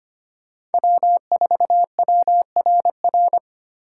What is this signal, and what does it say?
Morse code, heard as a keyed tone, about 32 wpm, 700 Hz. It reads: W4WRR